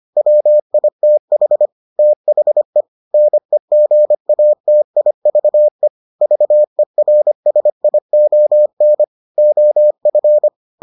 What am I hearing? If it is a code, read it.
Morse code, heard as a keyed tone, about 25 wpm, 600 Hz. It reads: WITH THE NEGATIVE VERSION OF